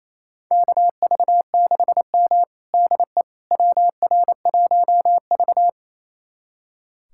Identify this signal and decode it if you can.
Morse code, heard as a keyed tone, about 28 wpm, 700 Hz. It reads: KV6M DE WR1V
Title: USA Callsign DE Callsign - 4 Characters - 28wpm